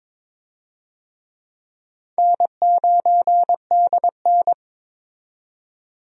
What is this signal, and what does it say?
Morse code, heard as a keyed tone, about 22 wpm, 700 Hz. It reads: N9DN
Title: USA Call Signs - 4 Characters - 22wpm